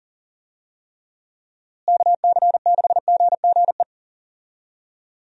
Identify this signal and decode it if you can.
Morse code, heard as a keyed tone, about 40 wpm, 700 Hz. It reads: KC6GGE
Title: USA Call Signs - 6 Characters - Repeated 3x - 40wpm